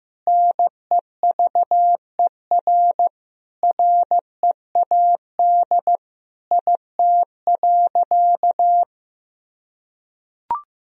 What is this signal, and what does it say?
Morse code, heard as a keyed tone, about 16 wpm, 700 Hz. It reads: NEVER READ IT.
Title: Sentences from Top 1000 Words - Review Part 4 - 15wpm